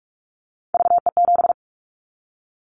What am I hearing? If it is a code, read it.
Morse code, heard as a keyed tone, about 44 wpm, 700 Hz. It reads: VE7
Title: Callsign Prefixes for Top 50 DXCC - 45wpm